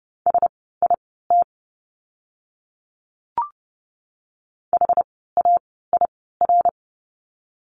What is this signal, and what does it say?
Morse code, heard as a keyed tone, about 30 wpm, spaced as for 17 wpm, 700 Hz. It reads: SIT HAIR
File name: Lesson 17f - Letter u - Words 30wpm